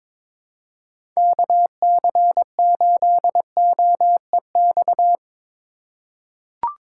Morse code, heard as a keyed tone, about 22 wpm, 700 Hz. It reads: KC8OEX
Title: USA Call Signs - 6 Characters - 2x Character Spacing - 22wpm